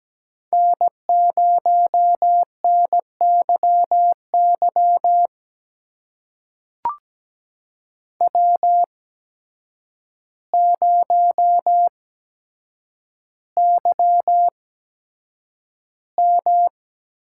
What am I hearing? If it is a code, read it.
Morse code, heard as a keyed tone, about 17 wpm, spaced as for 8 wpm, 700 Hz. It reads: N0NYY W0YM
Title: USA Call Signs - 8x Character Spacing - 17wpm